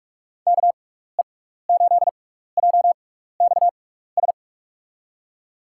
Morse code, heard as a keyed tone, about 46 wpm, spaced as for 16 wpm, 700 Hz. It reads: KE8JXS
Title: USA Call Signs - 6 Characters - 6x Character Spacing - 45wpm